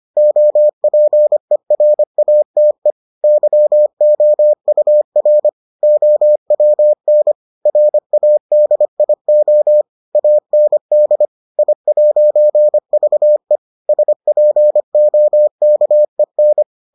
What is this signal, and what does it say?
Morse code, heard as a keyed tone, about 25 wpm, 600 Hz. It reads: OPERATE YOUR OWN RADIO AND I'VE SPOKEN